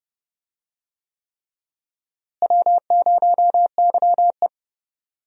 Morse code, heard as a keyed tone, about 30 wpm, 700 Hz. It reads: W0YE
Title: USA Call Signs - 4 Characters - Repeated 3x - 30wpm